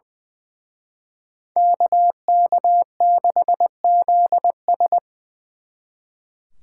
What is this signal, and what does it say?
Morse code, heard as a keyed tone, about 24 wpm, 700 Hz. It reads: KK6ZS
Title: Speed Racing - USA 2x2 Call Signs 20wpm